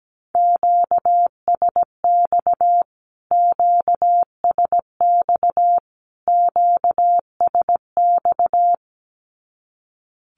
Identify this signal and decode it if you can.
Morse code, heard as a keyed tone, about 17 wpm, 700 Hz. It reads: QSX QSX QSX
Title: Q-Codes Repeated 3x 17wpm